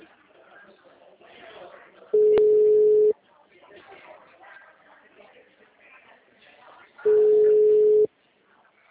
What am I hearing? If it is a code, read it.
Ringback tone (Europe, 425 Hz), with speech.